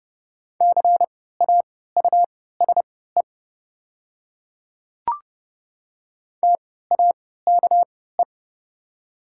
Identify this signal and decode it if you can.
Morse code, heard as a keyed tone, about 30 wpm, spaced as for 17 wpm, 700 Hz. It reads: CAUSE TAKE
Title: 3x Character Spacing - Top 200 Words - 30wpm